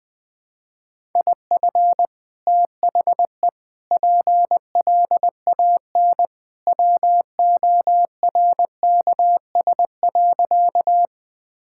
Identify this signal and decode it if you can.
Morse code, heard as a keyed tone, about 20 wpm, 700 Hz. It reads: IF THE PLAN WORKS.
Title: Sentences with 2-Word Phrases - 20wpm